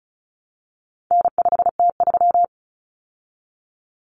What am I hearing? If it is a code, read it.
Morse code, heard as a keyed tone, about 34 wpm, 700 Hz. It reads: N5T3
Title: Common Cut Numbers - 4-Digit 35wpm